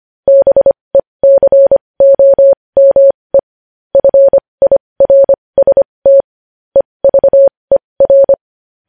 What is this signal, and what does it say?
Morse code, heard as a keyed tone, about 25 wpm, spaced as for 20 wpm, 550 Hz. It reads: BECOME FIRST EVER
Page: news headlines in morse code at 20 wpm